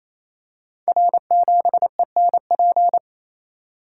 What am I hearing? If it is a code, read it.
Morse code, heard as a keyed tone, about 29 wpm, 700 Hz. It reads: R7ENP
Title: Code Groups - 5 Letter:Numbers - 28wpm